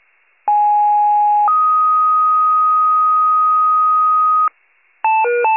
Two-tone page, heard as a one-second tone, then a three-second tone, about 830 then 1250 Hz. It is heard from 0.5 to 4.5 s.
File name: wcalert